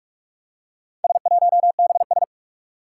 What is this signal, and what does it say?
Morse code, heard as a keyed tone, about 44 wpm, 700 Hz. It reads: S1BS
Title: Four Challenging Letter-Number Combinations 45wpm